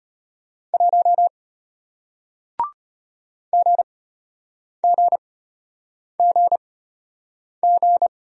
Morse code, heard as a keyed tone, about 34 wpm, spaced as for 8 wpm, 700 Hz. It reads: 1 GGGG